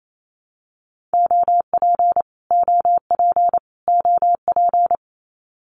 Morse code, heard as a keyed tone, about 28 wpm, 700 Hz. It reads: OP OP OP
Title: Abbreviations Repeated 3x - 28wpm